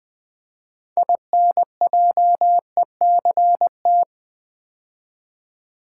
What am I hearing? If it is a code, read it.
Morse code, heard as a keyed tone, about 20 wpm, 700 Hz. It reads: INJECT
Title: Build A Word - 2 Letters At A Time - 6-Letter Words - 20wpm